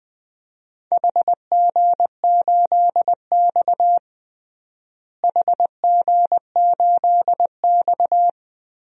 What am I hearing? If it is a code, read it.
Morse code, heard as a keyed tone, about 21 wpm, 700 Hz. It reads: HG8X HG8X
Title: DXCC-100 Call Signs - 4 Characters - Repeated 3x - 20wpm